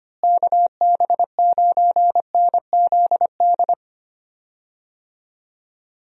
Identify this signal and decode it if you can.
Morse code, heard as a keyed tone, about 25 wpm, 700 Hz. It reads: KB9NZD
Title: USA Call Signs - 6 Characters - Repeated 2x - 25wpm